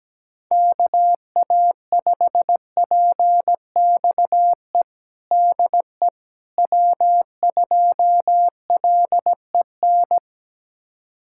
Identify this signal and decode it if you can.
Morse code, heard as a keyed tone, about 17 wpm, 700 Hz. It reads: KA5PXE DE W2LEN
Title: USA Callsign DE Callsign 17wpm